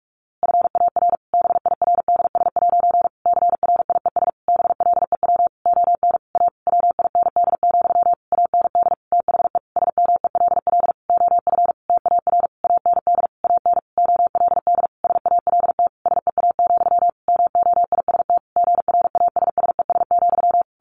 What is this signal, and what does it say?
Morse code, heard as a keyed tone, about 44 wpm, 700 Hz. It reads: FAR BIRDS' CRIES BLEW ON A WIND, AND THE SMELL OF TAR AND AN OLD SALT SEA, MOIST GRASSES,
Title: A Sound Of Thunder 45wpm